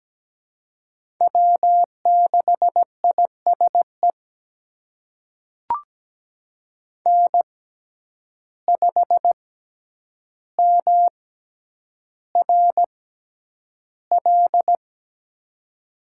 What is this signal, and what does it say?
Morse code, heard as a keyed tone, about 17 wpm, spaced as for 9 wpm, 700 Hz. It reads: W6ISE N5MRL
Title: USA Call Signs - 6x Character Spacing - 17wpm